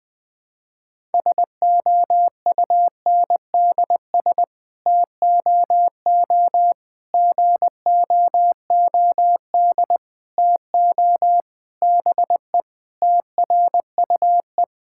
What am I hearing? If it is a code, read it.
Morse code, heard as a keyed tone, about 20 wpm, 700 Hz. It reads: SOUNDS TOO GOOD TO BE TRUE
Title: Sets of 9 Words - 20wpm